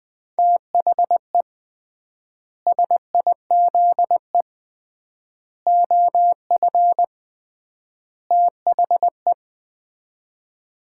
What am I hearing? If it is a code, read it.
Morse code, heard as a keyed tone, about 21 wpm, 700 Hz. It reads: THE SIZE OF THE
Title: 3x Word-Spacing - Sets of 4 Words - 20wpm